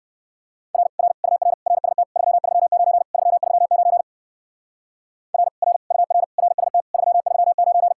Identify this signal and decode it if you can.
Morse code, heard as a keyed tone, about 53 wpm, 700 Hz. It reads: R R UR RST 339 339 R R UR RST 339
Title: Speed-Racing Common QSO Phrases Encore - 45wpm